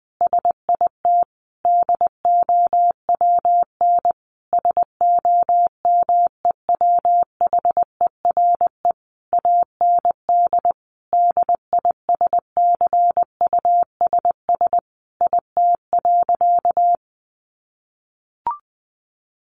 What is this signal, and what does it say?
Morse code, heard as a keyed tone, about 20 wpm, 700 Hz. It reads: SIT DOWN SOMEWHERE AND DISCUSS IT.
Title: Sentences from Top 1400 Words 20wpm